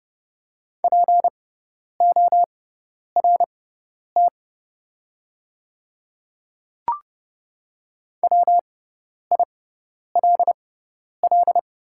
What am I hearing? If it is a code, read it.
Morse code, heard as a keyed tone, about 31 wpm, spaced as for 10 wpm, 700 Hz. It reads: PORT WILL
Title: 6x Character Spacing - Top 300 Words - 30wpm